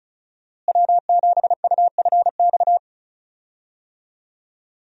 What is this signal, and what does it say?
Morse code, heard as a keyed tone, about 34 wpm, 700 Hz. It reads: W7UFX
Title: USA Call Signs - 5 Characters - Repeated 2x - 35wpm